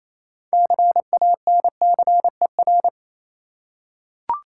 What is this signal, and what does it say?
Morse code, heard as a keyed tone, about 28 wpm, 700 Hz. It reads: CANCER
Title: Six Letter Words - 28wpm